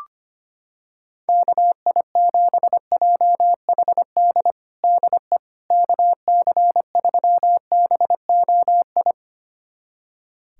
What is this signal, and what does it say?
Morse code, heard as a keyed tone, about 25 wpm, 700 Hz. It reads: KI7JHD DE KC3BOI
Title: USA Callsign DE Callsign 25wpm